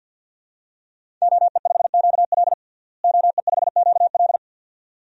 Morse code, heard as a keyed tone, about 51 wpm, 700 Hz. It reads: OE5QL OE5QL
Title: DXCC-100 Call Signs - 5 Characters - Repeated 2x - 50wpm